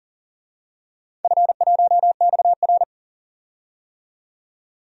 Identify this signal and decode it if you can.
Morse code, heard as a keyed tone, about 40 wpm, 700 Hz. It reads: F1XR
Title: DXCC-100 Call Signs - 4 Characters - 40wpm